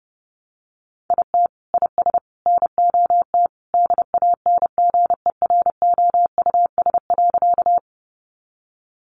Morse code, heard as a keyed tone, about 30 wpm, 700 Hz. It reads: IT IS NOT DANGEROUS.
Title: Sentences from Top 1200 Words 30wpm